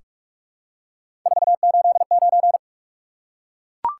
Morse code, heard as a keyed tone, about 44 wpm, 700 Hz. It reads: 489
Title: Top 180 QSO Elements 45wpm